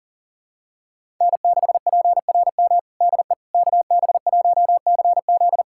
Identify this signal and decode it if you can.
Morse code, heard as a keyed tone, about 40 wpm, 700 Hz. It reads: N6PRM DE KB1CZ
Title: USA Callsign DE Callsign - 5 Characters - 40wpm